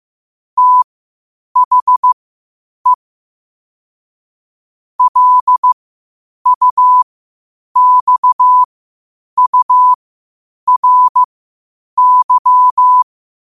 Morse code, heard as a keyed tone, about 15 wpm, spaced as for 8 wpm, 1000 Hz. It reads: THE LUXURY
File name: qotdfiles / 8wpm